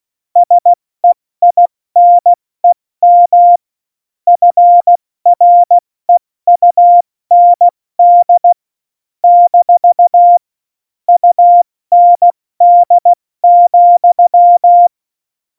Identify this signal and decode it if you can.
Morse code, heard as a keyed tone, about 16 wpm, 700 Hz. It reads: SEINEM FREUND - UND,
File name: Morsekurs / Witze / Witz 02. Juli 2019, 70 F80